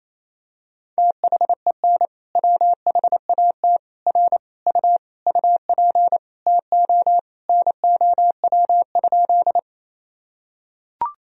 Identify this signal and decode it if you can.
Morse code, heard as a keyed tone, about 28 wpm, 700 Hz. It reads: THEN WHAT R U UP TO NOW?
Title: Sentences from Top 100 Words and Abbreviations 28wpm